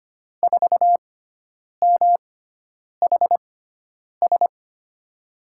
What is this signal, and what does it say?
Morse code, heard as a keyed tone, about 25 wpm, spaced as for 9 wpm, 700 Hz. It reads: 4MHS
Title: USA Call Signs - 6 Characters - 6x Character Spacing - 25wpm